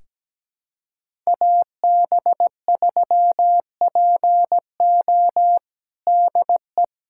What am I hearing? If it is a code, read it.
Morse code, heard as a keyed tone, about 17 wpm, 700 Hz. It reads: AB3PO DE
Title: USA Callsign DE Callsign - 5 Characters - 17wpm